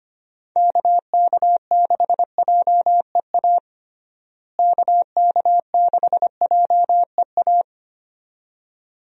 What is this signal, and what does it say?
Morse code, heard as a keyed tone, about 25 wpm, 700 Hz. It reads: KK6JEA KK6JEA
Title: USA Call Signs - 6 Characters - Repeated 2x - 25wpm